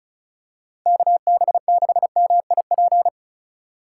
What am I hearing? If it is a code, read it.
Morse code, heard as a keyed tone, about 34 wpm, 700 Hz. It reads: KB6MIP
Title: USA Call Signs - 2x Character Spacing - 35wpm